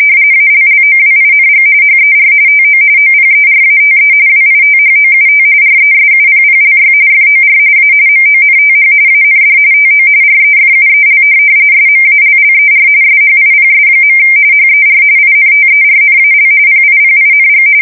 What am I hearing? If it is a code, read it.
RTTY, about 45.45 baud. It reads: ALLOWING HER TO OPERATE WITHOUT RESUPPLY FOR A MONTH AND TO REFUEL SMALLER SHIPS. HER INTERNAL PIPING
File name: fsk170 / USS-New-Jersey